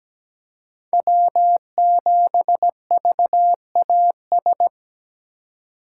Morse code, heard as a keyed tone, about 17 wpm, 700 Hz. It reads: W7VAS
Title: K1USN SST QSO Practice - 17wpm